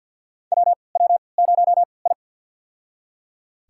Morse code, heard as a keyed tone, about 49 wpm, spaced as for 28 wpm, 700 Hz. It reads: WW0I
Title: USA Call Signs - 4 Characters - 3x Character Spacing - 50wpm